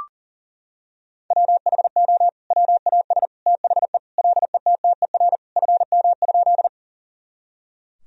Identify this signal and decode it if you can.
Morse code, heard as a keyed tone, about 40 wpm, 700 Hz. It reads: WHO WAS THE LETTER FM?
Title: Sentences from Top 300 Words and Abbreviations 40wpm